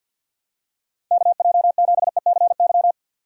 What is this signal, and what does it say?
Morse code, heard as a keyed tone, about 52 wpm, 700 Hz. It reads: KJ7ECY
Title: USA Call Signs - 6 Characters - 3x Character Spacing - 50wpm